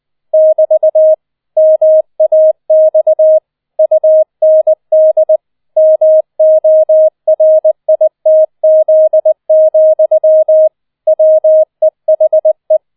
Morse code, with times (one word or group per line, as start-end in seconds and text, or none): 0.3-1.1 <BT>
1.6-3.4 MAX
3.8-5.4 UND
5.8-10.7 MORITZ,
11.1-12.8 WEHE